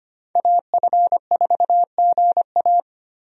Morse code, heard as a keyed tone, about 25 wpm, 700 Hz. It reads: AF4GA
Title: USA Call Signs - 5 Characters - Repeated 2x - 25wpm